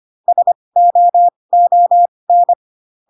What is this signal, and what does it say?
Morse code, heard as a keyed tone, about 25 wpm, spaced as for 20 wpm, 700 Hz. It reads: SOON
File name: Pinocchio / Pinocchio ch8 15 25 ik1hge